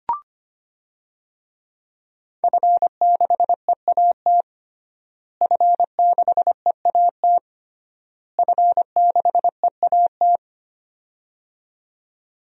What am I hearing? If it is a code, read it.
Morse code, heard as a keyed tone, about 25 wpm, 700 Hz. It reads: F6EAT F6EAT F6EAT
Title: DXCC-100 Call Signs - 5 Characters - Repeated 3x - 25wpm